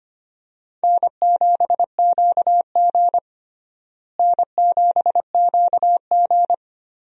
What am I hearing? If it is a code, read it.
Morse code, heard as a keyed tone, about 25 wpm, 700 Hz. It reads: N7QG N7QG
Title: USA Call Signs - 4 Characters - Repeated 2x - 25wpm